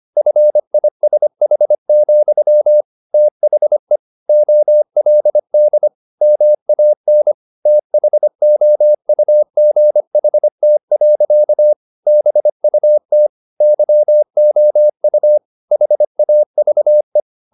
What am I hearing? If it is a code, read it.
Morse code, heard as a keyed tone, about 25 wpm, 600 Hz. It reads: FISH, THE OLD MAN THOUGHT. BUT YOU HAVE